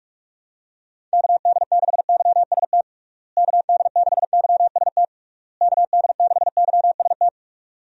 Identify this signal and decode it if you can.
Morse code, heard as a keyed tone, about 46 wpm, 700 Hz. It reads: KD6YST KD6YST KD6YST
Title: USA Call Signs - 6 Characters - Repeated 3x - 45wpm